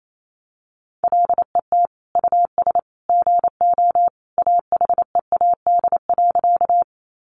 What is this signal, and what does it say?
Morse code, heard as a keyed tone, about 28 wpm, 700 Hz. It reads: LET US GO AHEAD.
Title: Sentences from Top 1100 Words 28wpm